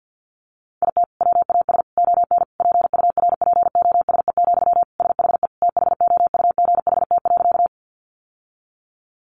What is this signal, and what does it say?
Morse code, heard as a keyed tone, about 50 wpm, 700 Hz. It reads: IT WAS ON PURPOSE, SHE THOUGHT.
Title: Sentences from Top 800 Words 50wpm